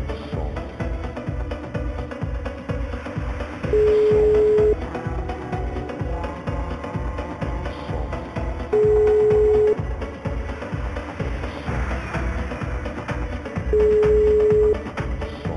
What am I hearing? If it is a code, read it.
Ringback tone (Europe, 425 Hz).